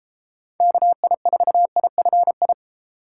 Morse code, heard as a keyed tone, about 30 wpm, 700 Hz. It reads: KI4IFI